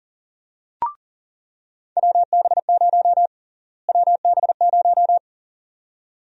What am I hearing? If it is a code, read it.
Morse code, heard as a keyed tone, about 40 wpm, 700 Hz. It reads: WB0 WB0
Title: USA Call Sign Prefixes - Repeated 2x - 40wpm